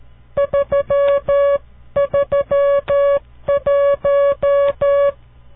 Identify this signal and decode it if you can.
Morse code, heard as a keyed tone, about 13 wpm, 560 Hz. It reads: 331
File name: CW Learning Programs / farnsworth / unit 5